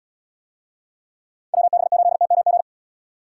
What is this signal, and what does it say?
Morse code, heard as a keyed tone, about 71 wpm, 700 Hz. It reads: WB8TMG